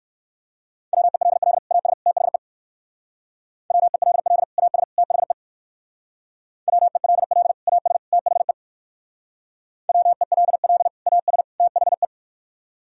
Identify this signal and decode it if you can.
Morse code, heard as a keyed tone, about 56 wpm, 700 Hz. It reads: WELL AS THE WELL AS THE WELL AS THE WELL AS THE
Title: Speed-Racing - Sets of 3 Words 45wpm